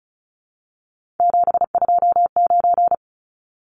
Morse code, heard as a keyed tone, about 34 wpm, 700 Hz. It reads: 729
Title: Numbers - 3-Digit 35wpm